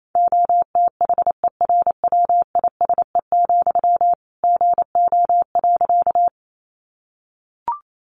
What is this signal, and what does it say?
Morse code, heard as a keyed tone, about 28 wpm, 700 Hz. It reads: OTHERWISE, GO.
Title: Sentences from Top 1200 Words 28wpm